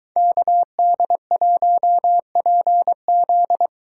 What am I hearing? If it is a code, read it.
Morse code, heard as a keyed tone, about 23 wpm, 700 Hz. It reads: KD1PZ